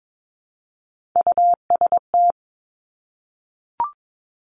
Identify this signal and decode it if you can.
Morse code, heard as a keyed tone, about 22 wpm, 690 Hz. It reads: UST
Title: Top Three Letter Combinations in Words 22wpm